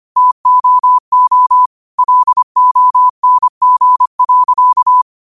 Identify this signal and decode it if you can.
Morse code, heard as a keyed tone, about 25 wpm, 1000 Hz. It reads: TOO LONG.
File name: qotdfiles / 25wpm